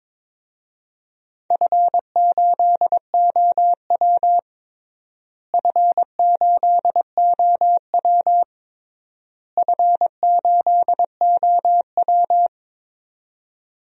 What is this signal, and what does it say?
Morse code, heard as a keyed tone, about 22 wpm, 700 Hz. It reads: F8OW F8OW F8OW
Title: DXCC-100 Call Signs - 4 Characters - Repeated 3x - 22wpm